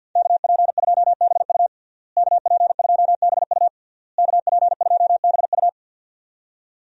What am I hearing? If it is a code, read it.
Morse code, heard as a keyed tone, about 50 wpm, 700 Hz. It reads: KP2BU KP2BU KP2BU
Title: USA Call Signs - 5 Characters - Repeated 3x - 50wpm